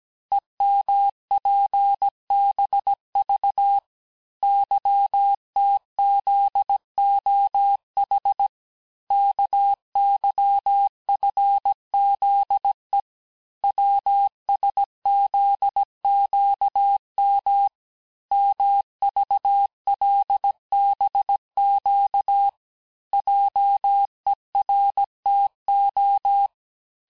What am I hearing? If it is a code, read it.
Morse code, heard as a keyed tone, about 17 wpm, 800 Hz. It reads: EMPBV YTZOH KYFZE WSZQM MVLBQ JERTO
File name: cours-du-lundi files / cours-1587-texte-chiffre-vitesse-16MPM